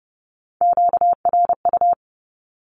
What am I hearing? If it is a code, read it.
Morse code, heard as a keyed tone, about 30 wpm, 700 Hz. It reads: QRU
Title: Q-Codes Repeated 3x 30wpm